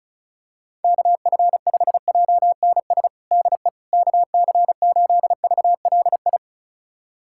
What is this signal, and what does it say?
Morse code, heard as a keyed tone, about 36 wpm, 700 Hz. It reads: KF5JNS DE KC8VLI
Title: USA Callsign DE Callsign - 6 Characters - 35wpm